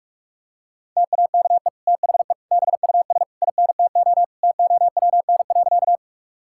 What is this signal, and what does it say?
Morse code, heard as a keyed tone, about 41 wpm, 700 Hz. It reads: TAKE THE BUS INTO TOWN.
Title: Sentences from Top 1000 Words - Review Part 6 - 45wpm